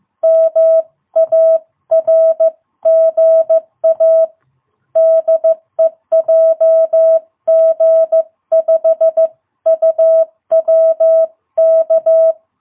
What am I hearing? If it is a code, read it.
Morse code, heard as a keyed tone, about 14 wpm, 660 Hz. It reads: MARGA DE JG5UWK
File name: jh8cbh / 2022cw